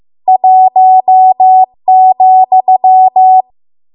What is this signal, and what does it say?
Morse code, heard as a keyed tone, about 15 wpm, 750 Hz. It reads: 1,
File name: Archive / 15 WPM / 15WPM